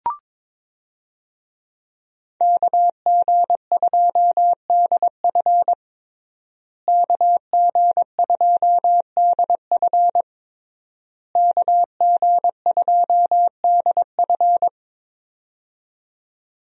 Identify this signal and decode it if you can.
Morse code, heard as a keyed tone, about 22 wpm, 700 Hz. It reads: KG2DF KG2DF KG2DF
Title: USA Call Signs - 5 Characters - Repeated 3x - 22wpm